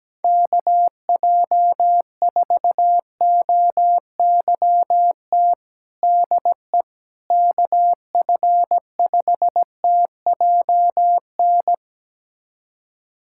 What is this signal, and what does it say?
Morse code, heard as a keyed tone, about 17 wpm, 700 Hz. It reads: KJ4OYT DE KF5TJN
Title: USA Callsign DE Callsign - 6 Characters - 17wpm